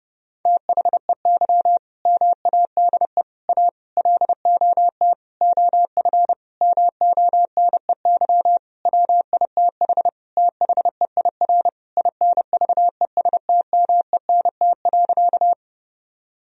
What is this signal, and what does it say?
Morse code, heard as a keyed tone, about 30 wpm, 700 Hz. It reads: THEY MADE A LOT OF MONEY WITH THEIR INVESTMENT.
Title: Sentences from Top 1000 Words - Review Part 4 - 30wpm